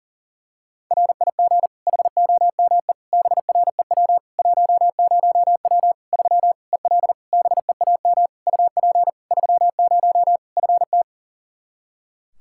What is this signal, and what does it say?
Morse code, heard as a keyed tone, about 40 wpm, 700 Hz. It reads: RIG HOME BREW 10W 3 EL BEAM UP 30 FT
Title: Practice QSO - Exchange 2 - Info - 40wpm